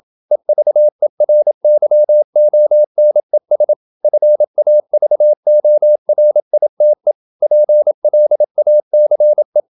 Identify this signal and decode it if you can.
Morse code, heard as a keyed tone, about 27 wpm, 600 Hz. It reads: EVERYONES FAVORITE PLACE
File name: Never 0027wpm